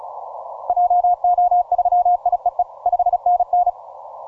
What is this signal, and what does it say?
Morse code, heard as a keyed tone, about 33 wpm, 710 Hz. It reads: JO3IEE 5NN